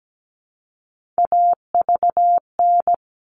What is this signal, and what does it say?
Morse code, heard as a keyed tone, about 17 wpm, 700 Hz. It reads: AVN
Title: All Cut Numbers - 3-Digit 17wpm